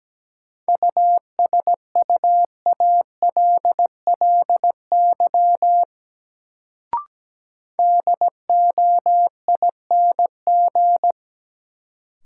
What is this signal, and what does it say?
Morse code, heard as a keyed tone, about 17 wpm, 700 Hz. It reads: USUALLY DOING
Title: Sentences from Top 1400 Words - New Word Drill Alt - 17wpm